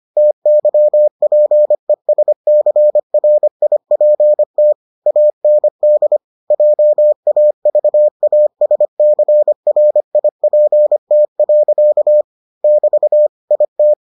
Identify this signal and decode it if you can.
Morse code, heard as a keyed tone, about 25 wpm, 600 Hz. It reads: TYPESCRIPT AND JAVASCRIPT. <BT> IT